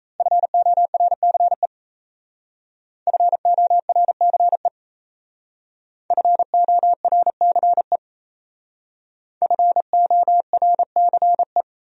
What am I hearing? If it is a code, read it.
Morse code, heard as a keyed tone, about 35 wpm, 700 Hz. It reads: FORCE FORCE FORCE FORCE